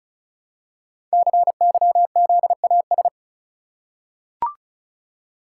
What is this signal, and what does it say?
Morse code, heard as a keyed tone, about 36 wpm, 700 Hz. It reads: CYZAS
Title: Code Groups - 5 Letters - 35wpm